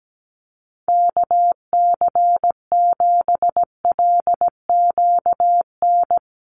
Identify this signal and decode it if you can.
Morse code, heard as a keyed tone, about 17 wpm, 700 Hz. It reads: KC7LQN
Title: USA Call Signs 17wpm